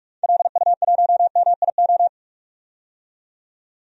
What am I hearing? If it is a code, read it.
Morse code, heard as a keyed tone, about 46 wpm, 700 Hz. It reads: LU1MIO
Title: DXCC-100 Call Signs - 6 Characters - 45wpm